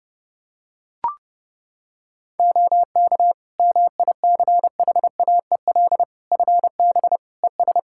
Morse code, heard as a keyed tone, about 30 wpm, 700 Hz. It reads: OK MICHAEL FB ES
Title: K1USN SST QSO Practice 30wpm